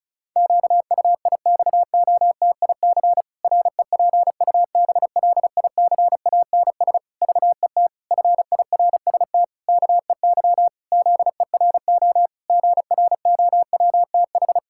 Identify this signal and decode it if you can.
Morse code, heard as a keyed tone, about 36 wpm, 700 Hz. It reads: QUIXOTIC REPUBLICANS VET FIRST KEY ZERO GROWTH
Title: Pangram Challenge v2 - 35wpm